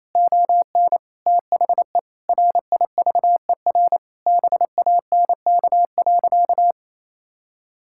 Morse code, heard as a keyed tone, about 28 wpm, 700 Hz. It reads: ON THE RIVER BANK.
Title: Sentences from Top 1000 Words - Review Part 6 - 28wpm